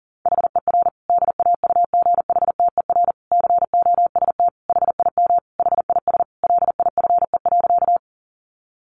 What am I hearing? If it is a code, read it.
Morse code, heard as a keyed tone, about 37 wpm, 700 Hz. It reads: HER DAUGHTER COST HIM HIS LIFE.
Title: Sentences from Top 1500 Words 40wpm